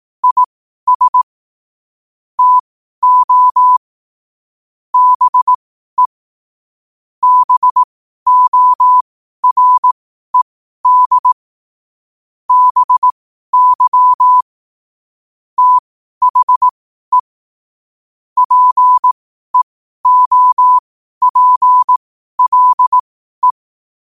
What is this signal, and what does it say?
Morse code, heard as a keyed tone, about 18 wpm, spaced as for 12 wpm, 1000 Hz. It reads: IS TO BE BORED BY THE PEOPLE
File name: qotdfiles / 12wpm